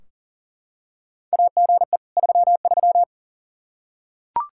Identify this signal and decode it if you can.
Morse code, heard as a keyed tone, about 38 wpm, 700 Hz. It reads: AGE 33
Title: K1USN SST QSO Practice 40wpm